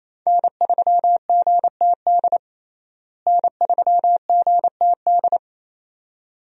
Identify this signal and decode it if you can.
Morse code, heard as a keyed tone, about 28 wpm, 700 Hz. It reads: N3GTD N3GTD
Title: USA Call Signs - 5 Characters - Repeated 2x - 28wpm